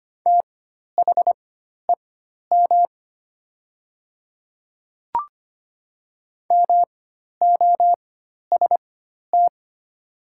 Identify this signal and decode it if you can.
Morse code, heard as a keyed tone, about 25 wpm, spaced as for 12 wpm, 700 Hz. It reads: THEM MOST
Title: 4x Character Spacing - Top 100 Words - 25wpm